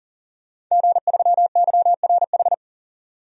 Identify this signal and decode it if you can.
Morse code, heard as a keyed tone, about 39 wpm, 700 Hz. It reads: G3YRH
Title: DXCC-100 Call Signs - 5 Characters - 40wpm